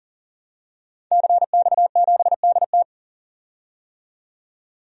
Morse code, heard as a keyed tone, about 40 wpm, 700 Hz. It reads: CX7DT
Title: DXCC-100 Call Signs - 5 Characters - 40wpm